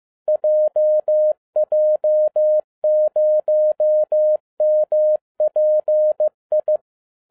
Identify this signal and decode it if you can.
Morse code, heard as a keyed tone, about 15 wpm, 610 Hz. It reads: JJ0MPI
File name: 2011-10-mp3 / n02-FIELD DAY CONTEST QRV FROM SAKAE VILLAGE 15wpm